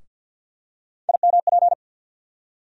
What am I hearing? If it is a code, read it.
Morse code, heard as a keyed tone, about 48 wpm, 700 Hz. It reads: IMP
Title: USA Call Sign Postfixes - 3 Letters 50wpm